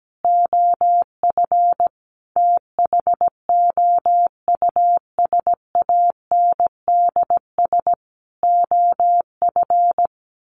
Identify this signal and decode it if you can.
Morse code, heard as a keyed tone, about 17 wpm, 700 Hz. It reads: OF THOUSANDS OF
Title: Sets of 4 Words 17wpm